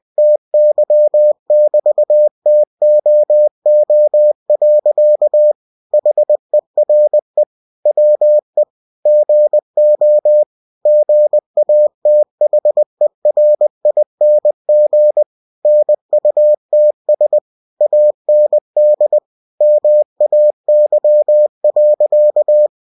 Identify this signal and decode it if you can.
Morse code, heard as a keyed tone, about 20 wpm, 600 Hz. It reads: TY <BT> TOO. HERE WE GO GATHERING NUTS AND MAY.
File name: cw-books / winnie-the-pooh / winnie-20wpm-page-0046